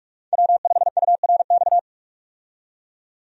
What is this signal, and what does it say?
Morse code, heard as a keyed tone, about 46 wpm, 700 Hz. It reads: W5URX
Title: USA Call Signs - 5 Characters - 45wpm